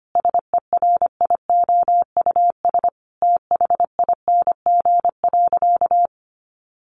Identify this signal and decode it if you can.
Morse code, heard as a keyed tone, about 25 wpm, 700 Hz. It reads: SERIOUS THING.
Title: Sentences from Top 1500 Words 25wpm